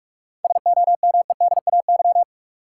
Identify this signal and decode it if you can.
Morse code, heard as a keyed tone, about 46 wpm, 700 Hz. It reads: SOMEDAY
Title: Sentences from Top 1600 Words - New Word Drill Alt - 45wpm